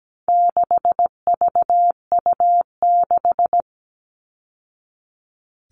Morse code, heard as a keyed tone, about 17 wpm, 700 Hz. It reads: 6VU6
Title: All Cut Numbers - 4-Digit 17wpm